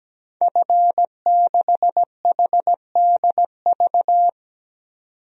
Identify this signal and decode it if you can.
Morse code, heard as a keyed tone, about 17 wpm, 700 Hz. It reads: F6HDV
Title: DXCC-100 Call Signs - 5 Characters - 17wpm